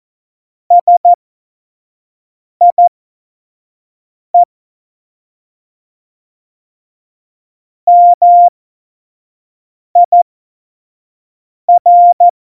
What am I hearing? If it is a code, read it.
Morse code, heard as a keyed tone, about 14 wpm, spaced as for 5 wpm, 700 Hz. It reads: SIE MIR
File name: Morsekurs / Witze / Witz 26 Mai 2020, 25 F70